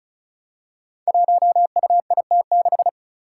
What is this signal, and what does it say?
Morse code, heard as a keyed tone, about 34 wpm, 700 Hz. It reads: 1UIT6
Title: Five Challenging Letter-Number Combinations 35wpm